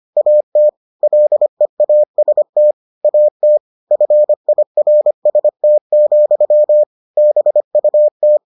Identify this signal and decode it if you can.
Morse code, heard as a keyed tone, about 25 wpm, 600 Hz. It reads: AT LEAST AT FIRST, BUT